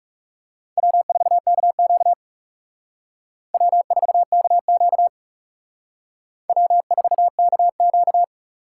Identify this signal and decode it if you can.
Morse code, heard as a keyed tone, about 37 wpm, 700 Hz. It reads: W4KQ W4KQ W4KQ
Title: Speed Racing - USA 1x2 Call Signs 30wpm